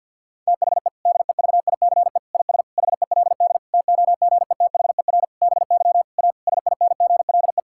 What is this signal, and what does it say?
Morse code, heard as a keyed tone, about 50 wpm, 700 Hz. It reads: THE DEVICE IS HELD TOGETHER BY A SINGLE
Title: Sentences from Top 1600 Words 50wpm